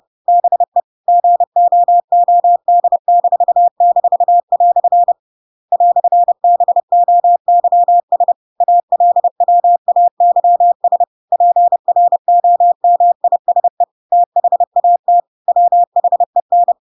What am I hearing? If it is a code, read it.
Morse code, heard as a keyed tone, about 30 wpm, 700 Hz. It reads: BE GOOD--" "BOYS ALWAYS PROMISE THAT WHEN
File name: Pinocchio / Pinocchio ch8 30 30 ik1hge